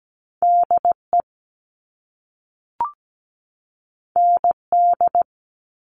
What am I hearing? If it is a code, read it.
Morse code, heard as a keyed tone, about 17 wpm, 700 Hz. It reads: DE ND
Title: All Cut Numbers - 2-Digit 17wpm